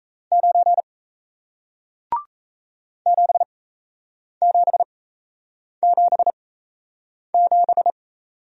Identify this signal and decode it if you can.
Morse code, heard as a keyed tone, about 37 wpm, spaced as for 8 wpm, 700 Hz. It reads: 9 7777